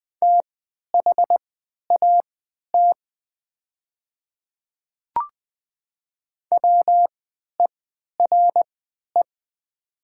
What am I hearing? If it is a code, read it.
Morse code, heard as a keyed tone, about 20 wpm, spaced as for 11 wpm, 700 Hz. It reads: THAT WERE
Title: 3x Character Spacing - Top 100 Words - 20wpm